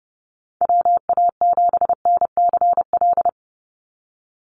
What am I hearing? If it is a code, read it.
Morse code, heard as a keyed tone, about 30 wpm, 700 Hz. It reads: WA7NCL
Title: NA SOTA Activator Callsigns 30wpm